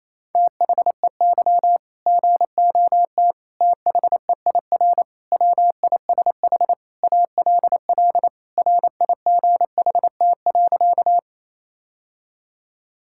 Morse code, heard as a keyed tone, about 28 wpm, 700 Hz. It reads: THEY GOT THEIR WISH ALL RIGHT.
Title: Sentences from Top 1600 Words 28wpm